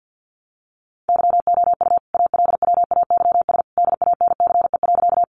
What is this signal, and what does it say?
Morse code, heard as a keyed tone, about 50 wpm, 700 Hz. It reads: YOU ALWAYS DANCE.
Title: Sentences from Top 700 Words 50wpm